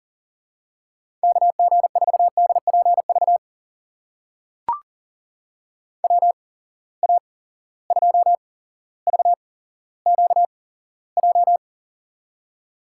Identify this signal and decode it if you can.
Morse code, heard as a keyed tone, about 39 wpm, spaced as for 17 wpm, 700 Hz. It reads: KG4DPV WA2VQJ
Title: USA Call Signs - 6 Characters - 8x Character Spacing - 40wpm